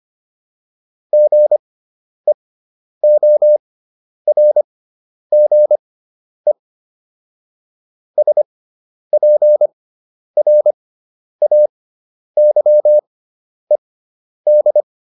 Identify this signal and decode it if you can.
Morse code, heard as a keyed tone, about 25 wpm, spaced as for 10 wpm, 600 Hz. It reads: GEORGE SPRAYED